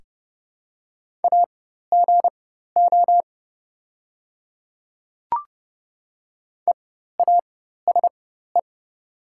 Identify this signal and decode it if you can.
Morse code, heard as a keyed tone, about 30 wpm, spaced as for 14 wpm, 690 Hz. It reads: AGO EASE
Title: 4x Character Spacing - Top 500 Words - 30wpm